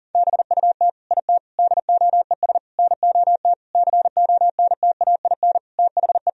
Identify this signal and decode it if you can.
Morse code, heard as a keyed tone, about 40 wpm, 700 Hz. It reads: BUT IT DOES NOT CONTAIN THE
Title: Sentences from Top 1000 Words - Review Part 2 - 40wpm